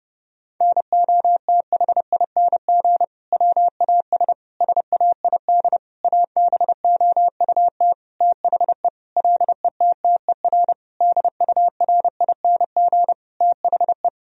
Morse code, heard as a keyed tone, about 30 wpm, 700 Hz. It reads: NOTHING WAS SAID ABOUT THE LETTER DURING THE
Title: Sentences from Top 1000 Words - Review Part 2 - 30wpm